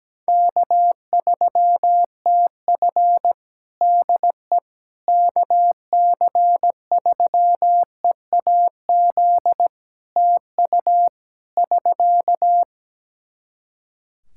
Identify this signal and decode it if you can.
Morse code, heard as a keyed tone, about 17 wpm, 700 Hz. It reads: K3TF DE KC3EAZ TU <SK>
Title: Practice QSO - Exchange 4 - Ending - 17wpm